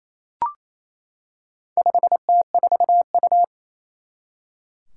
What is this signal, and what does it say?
Morse code, heard as a keyed tone, about 28 wpm, 700 Hz. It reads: 5T4U
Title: Four Challenging Letter-Number Combinations 28wpm